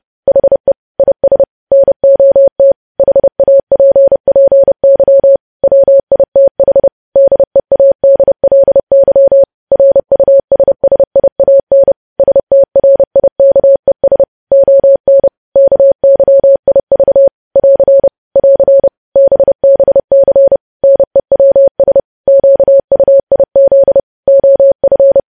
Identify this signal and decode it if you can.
Morse code, heard as a keyed tone, about 30 wpm, 550 Hz. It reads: HE IS NOT HAPPY WITH DEADLY RUSSIAN STRIKES ON KYIV <AR> <AR> BBC NEWS QUIZ OF
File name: podcast / News headlines-1x-30x30